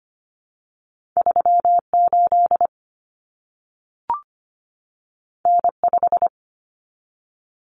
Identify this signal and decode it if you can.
Morse code, heard as a keyed tone, about 25 wpm, 700 Hz. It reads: 3 8 N 5
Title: Common Cut Numbers - 2-Digit 25wpm